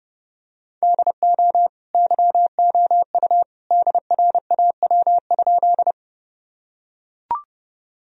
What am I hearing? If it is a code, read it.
Morse code, heard as a keyed tone, about 30 wpm, 700 Hz. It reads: DO YOU DRAW?
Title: Sentences from Top 1000 Words - Review Part 1 - 30wpm